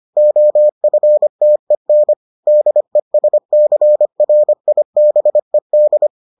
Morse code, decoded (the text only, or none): OFTEN DESCRIBED